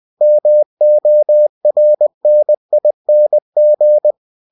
Morse code, heard as a keyed tone, about 20 wpm, 600 Hz. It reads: MORNING
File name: cw-books / winnie-the-pooh / winnie-20wpm-page-0024